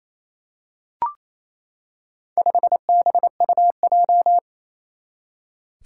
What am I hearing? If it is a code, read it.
Morse code, heard as a keyed tone, about 27 wpm, 700 Hz. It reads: 5BUJ
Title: Four Challenging Letter-Number Combinations 28wpm